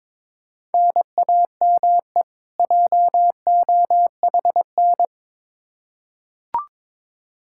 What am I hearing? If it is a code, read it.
Morse code, heard as a keyed tone, about 22 wpm, 700 Hz. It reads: NAME JOHN
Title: Practice QSO - Exchange 1 - Beginning - 22wpm